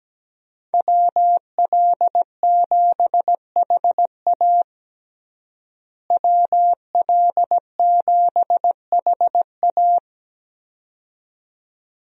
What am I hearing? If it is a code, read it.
Morse code, heard as a keyed tone, about 17 wpm, 700 Hz. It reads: WL7HA WL7HA
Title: USA Call Signs - 5 Characters - Repeated 2x - 17wpm